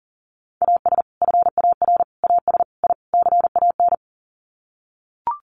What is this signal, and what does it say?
Morse code, heard as a keyed tone, about 40 wpm, 700 Hz. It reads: AS FAR AS I CAN
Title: Sets of 5 Words Encore 40wpm